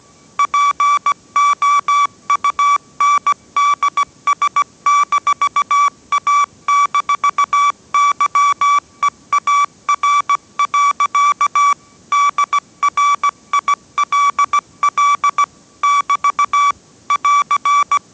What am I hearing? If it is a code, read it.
Morse code, heard as a keyed tone, about 17 wpm, 1150 Hz. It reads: POUNDS-A-YEAR. DRILL <BT> <AR>
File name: 18wpm